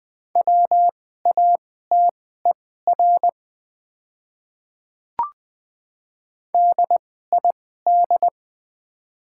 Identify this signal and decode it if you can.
Morse code, heard as a keyed tone, about 20 wpm, spaced as for 15 wpm, 700 Hz. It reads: WATER DID
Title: 2x Character Spacing - Top 100 Words - 20wpm